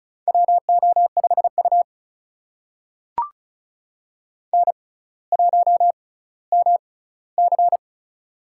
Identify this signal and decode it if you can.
Morse code, heard as a keyed tone, about 36 wpm, spaced as for 18 wpm, 700 Hz. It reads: WO5U N1MC
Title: USA Call Signs - 4 Characters - 6x Character Spacing - 35wpm